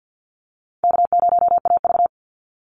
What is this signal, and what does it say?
Morse code, heard as a keyed tone, about 53 wpm, 700 Hz. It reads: K0AV
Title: Easy Call Signs Repeated 3x - 50wpm